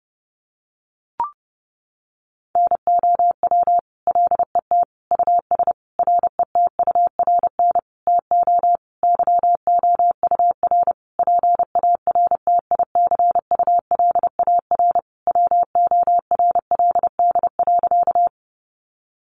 Morse code, heard as a keyed tone, about 30 wpm, 700 Hz. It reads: NOW LET US RETURN TO YOUR PARTICULAR WORLD.
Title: Sentences from Top 800 Words 30wpm